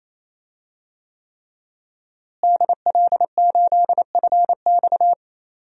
Morse code, heard as a keyed tone, about 27 wpm, 700 Hz. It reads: DL8FX
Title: DXCC-100 Call Signs - 5 Characters - Repeated 3x - 28wpm